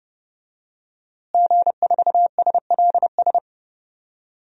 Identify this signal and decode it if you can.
Morse code, heard as a keyed tone, about 32 wpm, 700 Hz. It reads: G4SLS
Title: DXCC-100 Call Signs - 5 Characters - Repeated 3x - 30wpm